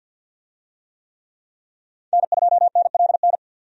Morse code, heard as a keyed tone, about 51 wpm, 700 Hz. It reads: N2NLN
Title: USA Call Signs - 5 Characters - Repeated 2x - 50wpm